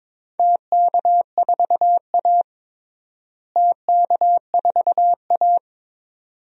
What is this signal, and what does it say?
Morse code, heard as a keyed tone, about 22 wpm, 700 Hz. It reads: TK4A TK4A
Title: DXCC-100 Call Signs - 4 Characters - Repeated 2x - 22wpm